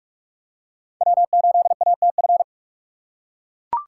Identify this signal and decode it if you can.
Morse code, heard as a keyed tone, about 42 wpm, 700 Hz. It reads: W8ATF